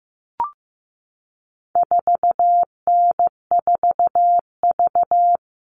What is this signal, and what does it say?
Morse code, heard as a keyed tone, about 15 wpm, 700 Hz. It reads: 4N4V
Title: All Cut Numbers - 4-Digit 15wpm